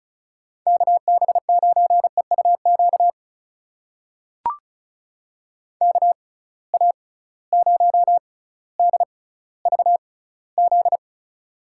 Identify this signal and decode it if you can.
Morse code, heard as a keyed tone, about 34 wpm, spaced as for 18 wpm, 700 Hz. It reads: KB9EUQ KA0DVZ
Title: USA Call Signs - 6 Characters - 6x Character Spacing - 35wpm